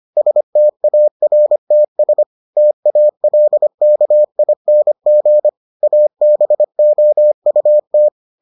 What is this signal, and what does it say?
Morse code, heard as a keyed tone, about 25 wpm, 600 Hz. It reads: STARTS TALKING ABOUT